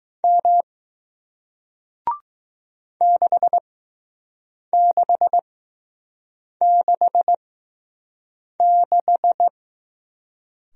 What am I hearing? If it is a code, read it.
Morse code, heard as a keyed tone, about 20 wpm, spaced as for 6 wpm, 700 Hz. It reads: M 6666